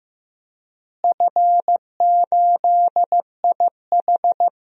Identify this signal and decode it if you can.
Morse code, heard as a keyed tone, about 15 wpm, 700 Hz. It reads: F8IH
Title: DXCC-100 Call Signs - 4 Characters - Repeated 2x - 15wpm